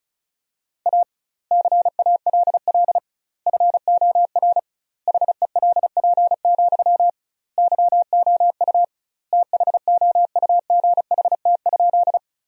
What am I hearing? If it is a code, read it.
Morse code, heard as a keyed tone, about 34 wpm, 700 Hz. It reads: A CALL FOR HELP, YOU THOUGHT?
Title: 2x Word-Spacing - Sentences from Top 300 Words - 35wpm